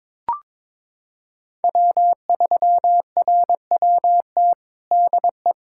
Morse code, heard as a keyed tone, about 22 wpm, 700 Hz. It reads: W3RWT DE
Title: USA Callsign DE Callsign - 5 Characters - 22wpm